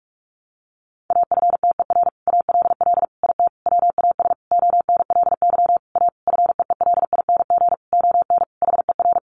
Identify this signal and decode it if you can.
Morse code, heard as a keyed tone, about 44 wpm, 700 Hz. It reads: AFTER ALL IT WAS ONLY A FEELING ON HER
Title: Sentences from Top 1200 Words 45wpm